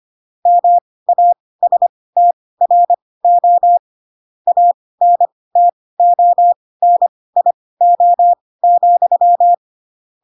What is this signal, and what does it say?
Morse code, heard as a keyed tone, about 25 wpm, spaced as for 18 wpm, 700 Hz. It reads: MASTRO ANTONIO,
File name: Pinocchio / Pinocchio ch2 12 25 ik1hge